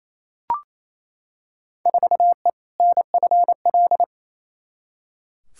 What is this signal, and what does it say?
Morse code, heard as a keyed tone, about 28 wpm, 700 Hz. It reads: 4E NFL
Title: Field Day Exchanges Encore - 28wpm